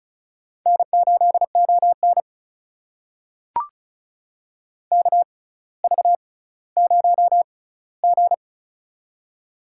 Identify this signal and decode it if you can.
Morse code, heard as a keyed tone, about 34 wpm, spaced as for 18 wpm, 700 Hz. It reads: N8ON KV0G
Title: USA Call Signs - 4 Characters - 6x Character Spacing - 35wpm